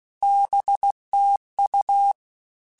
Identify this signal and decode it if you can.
Morse code, heard as a keyed tone, about 16 wpm, 800 Hz. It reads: BT U
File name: QSOs / QSOs-mp3s / WD0FBT 1@15